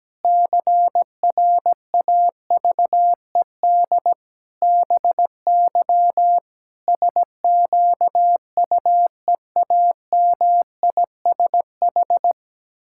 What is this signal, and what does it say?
Morse code, heard as a keyed tone, about 17 wpm, 700 Hz. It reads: CRAVED BY SQUEAMISH
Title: Pangram Challenge v2 - 17wpm